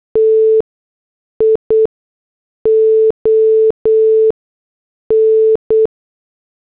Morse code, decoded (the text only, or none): TION